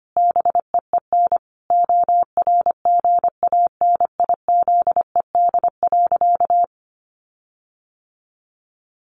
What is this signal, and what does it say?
Morse code, heard as a keyed tone, about 25 wpm, 700 Hz. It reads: BEEN ORGANIZED.
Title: Sentences from Top 1400 Words 25wpm